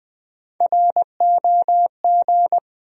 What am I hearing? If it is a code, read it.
Morse code, heard as a keyed tone, about 20 wpm, 700 Hz. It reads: ROG
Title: USA Common Names - Max 3 Letters - 20wpm